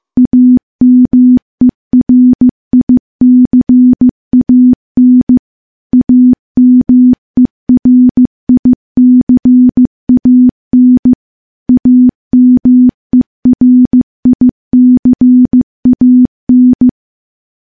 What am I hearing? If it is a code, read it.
Morse code, heard as a keyed tone, about 15 wpm, 250 Hz. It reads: AMERICAN AMERICAN AMERICAN
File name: morse-training-15WPM